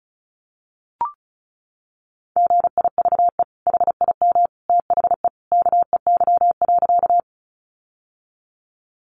Morse code, heard as a keyed tone, about 34 wpm, 700 Hz. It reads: GIVE HIM THE KEY.
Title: Sentences from Top 600 Words 35wpm